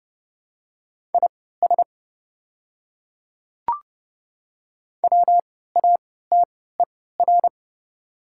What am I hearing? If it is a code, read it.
Morse code, heard as a keyed tone, about 30 wpm, spaced as for 17 wpm, 700 Hz. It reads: IS WATER
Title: 3x Character Spacing - Top 100 Words - 30wpm